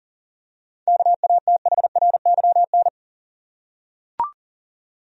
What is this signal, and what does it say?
Morse code, heard as a keyed tone, about 40 wpm, 700 Hz. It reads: KATHRYN
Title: USA Common Names - 40wpm